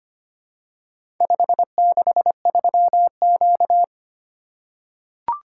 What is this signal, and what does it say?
Morse code, heard as a keyed tone, about 25 wpm, 700 Hz. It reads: 563Q
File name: practice / 25wpm / Four Letter-Number Combinations - 25wpm